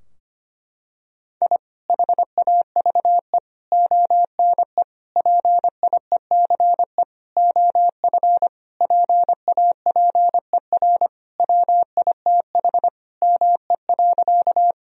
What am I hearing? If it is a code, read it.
Morse code, heard as a keyed tone, about 25 wpm, 700 Hz. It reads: I HAVE ONE PIECE OF PAPER WITH ME.
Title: Sentences from Top 1000 Words - Review Part 2 - 25wpm